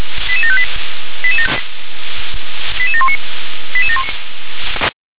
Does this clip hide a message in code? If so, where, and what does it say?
Yes: five-tone selective call (ZVEI) from 0.3 to 3.2 s, heard as quick runs of five tones, tones of about 70 ms. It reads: 88758 88719